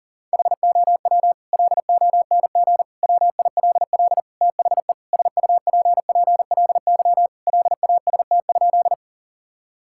Morse code, heard as a keyed tone, about 40 wpm, 700 Hz. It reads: HOW LONG WILL THE SUPPLY LAST?
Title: Sentences from Top 1000 Words - Review Part 5 - 40wpm